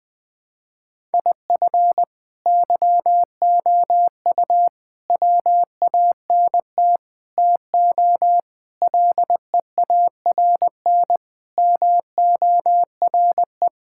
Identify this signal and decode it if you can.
Morse code, heard as a keyed tone, about 21 wpm, 700 Hz. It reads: IF YOU WANT TO LEARN MORE
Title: Sentences with 7-Word Phrases - 20wpm